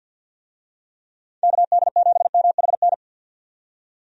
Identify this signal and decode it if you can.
Morse code, heard as a keyed tone, about 52 wpm, 700 Hz. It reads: KD7MHN
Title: USA Call Signs - 6 Characters - 50wpm